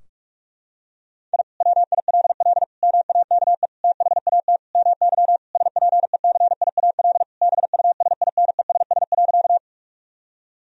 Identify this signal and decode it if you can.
Morse code, heard as a keyed tone, about 41 wpm, 700 Hz. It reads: I WILL MAKE THAT MY SPECIAL BUSINESS.
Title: Sentences from Top 1000 Words - Review Part 3 - 45wpm